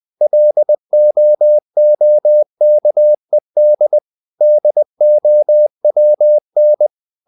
Morse code, heard as a keyed tone, about 20 wpm, 600 Hz. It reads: LOOKED DOWN
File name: the old man and the sea / the old man and the sea part 1 20wpm